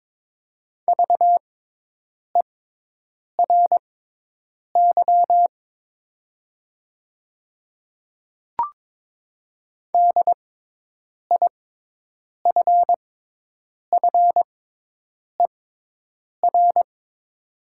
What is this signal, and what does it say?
Morse code, heard as a keyed tone, about 21 wpm, spaced as for 8 wpm, 700 Hz. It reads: VERY DIFFER
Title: 6x Character Spacing - Top 300 Words - 22wpm